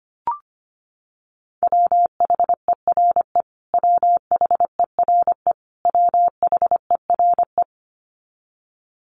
Morse code, heard as a keyed tone, about 25 wpm, 700 Hz. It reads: WHERE WHERE WHERE
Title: Top 300 Words Repeated 25wpm